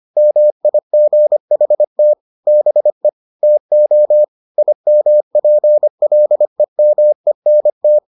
Morse code, heard as a keyed tone, about 25 wpm, 600 Hz. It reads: MIGHT BE TO IMPLEMENT